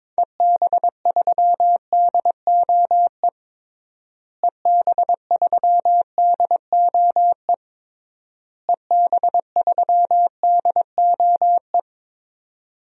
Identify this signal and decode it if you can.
Morse code, heard as a keyed tone, about 22 wpm, 700 Hz. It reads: EB3DOE EB3DOE EB3DOE
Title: DXCC-100 Call Signs - 6 Characters - Repeated 3x - 22wpm